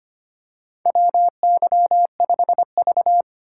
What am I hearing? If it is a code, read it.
Morse code, heard as a keyed tone, about 25 wpm, 700 Hz. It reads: WY5V
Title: USA Call Signs - 6x Character Spacing - 25wpm